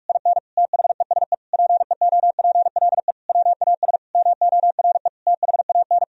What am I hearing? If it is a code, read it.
Morse code, heard as a keyed tone, about 46 wpm, 700 Hz. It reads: IN THESE PEOPLE WAS MORE THAN
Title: Sentences from Top 1000 Words - Review Part 2 - 45wpm